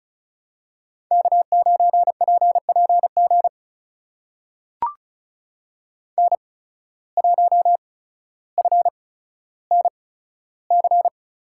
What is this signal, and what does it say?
Morse code, heard as a keyed tone, about 34 wpm, spaced as for 15 wpm, 690 Hz. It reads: K9PPG N1FNC
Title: USA Call Signs - 5 Characters - 8x Character Spacing - 35wpm